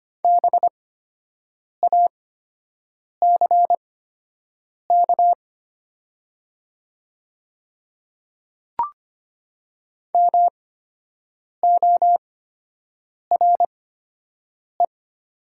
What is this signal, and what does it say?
Morse code, heard as a keyed tone, about 25 wpm, spaced as for 7 wpm, 700 Hz. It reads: BACK MORE
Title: 8x Character Spacing - Top 200 Words - 25wpm